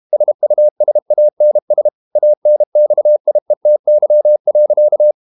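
Morse code, heard as a keyed tone, about 32 wpm, 600 Hz. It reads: SUSANS ANXIETY.
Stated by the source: Morse code practice at 32 wpm from CW med Gnister, reading Ken Follett's novel Never in English.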